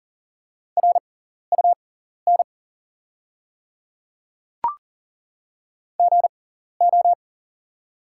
Morse code, heard as a keyed tone, about 40 wpm, spaced as for 14 wpm, 690 Hz. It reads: RUN GO
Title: 6x Character Spacing - Top 300 Words - 40wpm